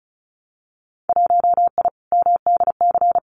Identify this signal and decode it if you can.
Morse code, heard as a keyed tone, about 34 wpm, 700 Hz. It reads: 1I MDC
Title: Winter Field Day Exchanges 35wpm